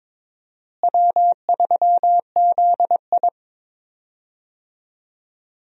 Morse code, heard as a keyed tone, about 22 wpm, 700 Hz. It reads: W3ZI
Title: USA Call Signs - 4 Characters - Repeated 2x - 22wpm